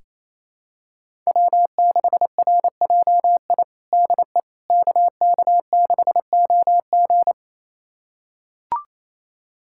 Morse code, heard as a keyed tone, about 28 wpm, 700 Hz. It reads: W6RJI DE KK6OG
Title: USA Callsign DE Callsign - 5 Characters - 28wpm